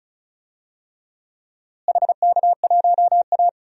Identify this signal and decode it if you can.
Morse code, heard as a keyed tone, about 34 wpm, 700 Hz. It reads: HK1A
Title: DXCC-100 Call Signs - 4 Characters - Repeated 3x - 35wpm